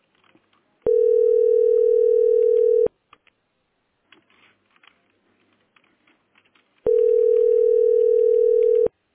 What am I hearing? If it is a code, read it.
Ringback tone (North America), 440 + 480 Hz.